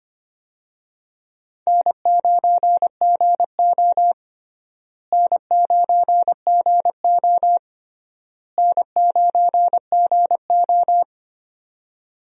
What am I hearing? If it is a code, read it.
Morse code, heard as a keyed tone, about 25 wpm, 700 Hz. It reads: N9GO N9GO N9GO
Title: USA Call Signs - 4 Characters - Repeated 3x - 25wpm